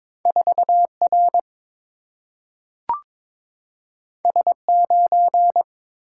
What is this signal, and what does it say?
Morse code, heard as a keyed tone, about 22 wpm, 690 Hz. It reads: 4R S9
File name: practice / 22wpm / Two Letter-Number Combinations - 22wpm